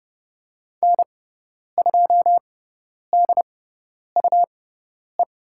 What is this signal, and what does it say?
Morse code, heard as a keyed tone, about 30 wpm, spaced as for 10 wpm, 700 Hz. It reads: N2DUE
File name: Lesson 18g - Letter m - Call Signs 30wpm